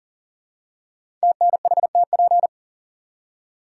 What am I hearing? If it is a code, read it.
Morse code, heard as a keyed tone, about 40 wpm, 700 Hz. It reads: TNHTP
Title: Code Groups - 5 Letters - 40wpm